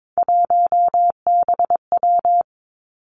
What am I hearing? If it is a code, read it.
Morse code, heard as a keyed tone, about 22 wpm, 700 Hz. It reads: 1BW
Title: Three Challenging Letter-Number Combinations 22wpm